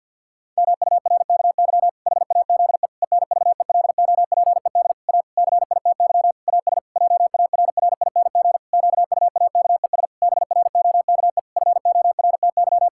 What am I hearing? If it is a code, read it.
Morse code, heard as a keyed tone, about 52 wpm, 700 Hz. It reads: MURKY HAZE ENVELOPED A CITY AS JARRING QUAKES BROKE FORTY
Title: Pangram Challenge v2 - 50wpm